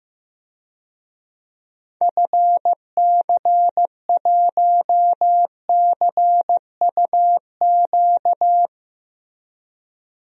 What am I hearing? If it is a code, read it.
Morse code, heard as a keyed tone, about 15 wpm, 700 Hz. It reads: FC1CUQ
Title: DXCC-100 Call Signs - 6 Characters - Repeated 3x - 15wpm